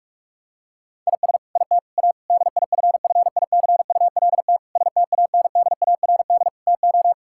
Morse code, heard as a keyed tone, about 45 wpm, 700 Hz. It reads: IS IT A DIFFICULT STANDARD TO
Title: Sentences from Top 1000 Words - Review Part 6 - 45wpm